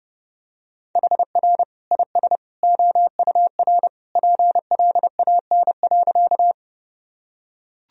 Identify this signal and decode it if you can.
Morse code, heard as a keyed tone, about 30 wpm, 700 Hz. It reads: HR IS OUR PLAN.
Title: Sentences from Top 500 Words and Abbreviations 30wpm